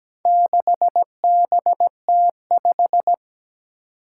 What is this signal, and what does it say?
Morse code, heard as a keyed tone, about 17 wpm, 700 Hz. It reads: 6BT5
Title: Four Challenging Letter-Number Combinations 17wpm